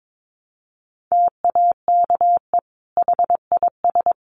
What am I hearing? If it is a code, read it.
Morse code, heard as a keyed tone, about 22 wpm, 700 Hz. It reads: TAKE HIS
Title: Sentences from Top 1300 Words 22wpm